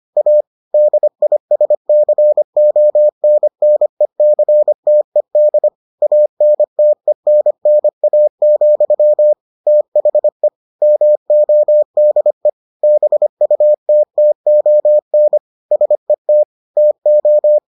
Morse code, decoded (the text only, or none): A DISCONNECTED ANTENNA, THE MODE BUTTON SET TO